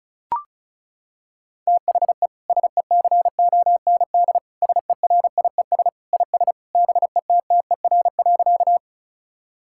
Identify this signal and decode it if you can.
Morse code, heard as a keyed tone, about 36 wpm, 700 Hz. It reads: THE SECOND SERIES IS BETTER.
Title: Sentences from Top 1000 Words - Review Part 5 - 35wpm